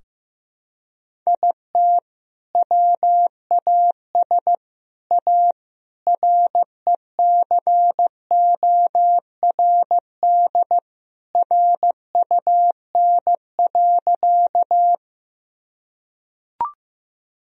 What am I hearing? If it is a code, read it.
Morse code, heard as a keyed tone, about 15 wpm, 700 Hz. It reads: IT WAS A RECORD RUN.
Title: Sentences from Top 1000 Words - Review Part 3 - 15wpm